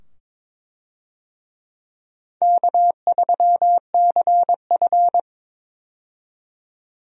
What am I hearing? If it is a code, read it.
Morse code, heard as a keyed tone, about 22 wpm, 700 Hz. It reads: K3CF
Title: USA Call Signs - 4 Characters - 22wpm